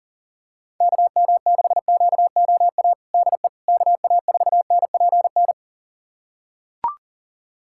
Morse code, heard as a keyed tone, about 40 wpm, 700 Hz. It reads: KM6QOA DE KA4NPN
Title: USA Callsign DE Callsign - 6 Characters - 40wpm